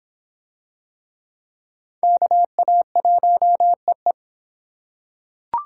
Morse code, heard as a keyed tone, about 25 wpm, 700 Hz. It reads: KA1EE